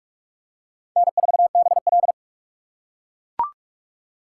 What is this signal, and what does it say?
Morse code, heard as a keyed tone, about 43 wpm, 700 Hz. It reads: N4BL
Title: USA Call Signs - 4 Characters - 2x Character Spacing - 45wpm